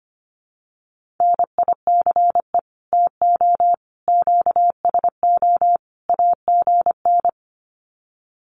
Morse code, heard as a keyed tone, about 25 wpm, 700 Hz. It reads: NICE TO QSO AGN
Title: Common QSO Phrases 25wpm